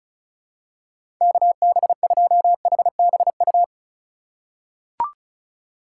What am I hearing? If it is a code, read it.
Morse code, heard as a keyed tone, about 36 wpm, 700 Hz. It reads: KB2HBU
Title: USA Call Signs - 6 Characters - 3x Character Spacing - 35wpm